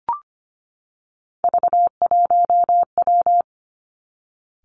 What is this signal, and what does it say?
Morse code, heard as a keyed tone, about 25 wpm, 700 Hz. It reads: V1W
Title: Three Challenging Letter-Number Combinations 25wpm